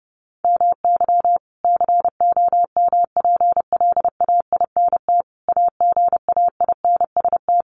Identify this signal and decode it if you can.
Morse code, heard as a keyed tone, about 30 wpm, 700 Hz. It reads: MY COMPLAINT AGAINST
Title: Sentences from Top 1400 Words 30wpm